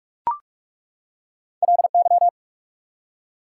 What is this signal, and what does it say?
Morse code, heard as a keyed tone, about 46 wpm, 690 Hz. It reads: LY
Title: Prefixes and Suffixes 45wpm